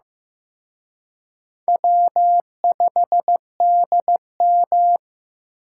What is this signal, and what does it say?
Morse code, heard as a keyed tone, about 18 wpm, 700 Hz. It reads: W5DM
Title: Speed Racing - USA Call Signs 15wpm